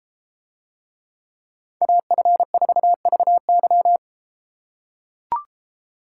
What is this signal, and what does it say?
Morse code, heard as a keyed tone, about 31 wpm, 700 Hz. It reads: AF4VY